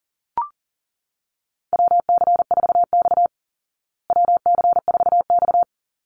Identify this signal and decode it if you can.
Morse code, heard as a keyed tone, about 40 wpm, 700 Hz. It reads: WC4X WC4X
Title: Call Signs Repeated 3x - 40wpm